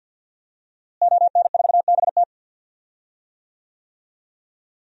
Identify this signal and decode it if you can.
Morse code, heard as a keyed tone, about 51 wpm, 700 Hz. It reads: ON4BT
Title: DXCC-100 Call Signs - 5 Characters - Repeated 2x - 50wpm